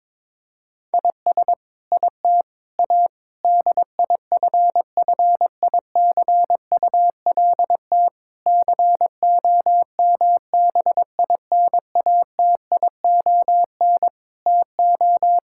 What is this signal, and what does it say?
Morse code, heard as a keyed tone, about 22 wpm, 700 Hz. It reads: IS IT A DIFFICULT COMBINATION TO
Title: Sentences from Top 1000 Words - Review Part 6 - 22wpm